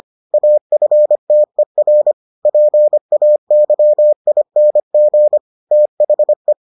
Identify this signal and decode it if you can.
Morse code, heard as a keyed tone, about 25 wpm, 600 Hz. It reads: AFTER PAYING THE